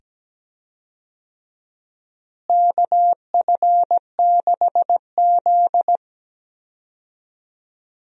Morse code, heard as a keyed tone, about 17 wpm, 700 Hz. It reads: KF6Z
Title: USA Call Signs - 4 Characters - Repeated 2x - 17wpm